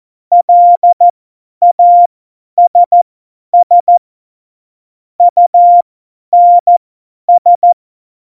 Morse code, heard as a keyed tone, about 14 wpm, spaced as for 10 wpm, 700 Hz. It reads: LASS UNS
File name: Morsekurs / Witze / Witz 25. September 2018, 50 F70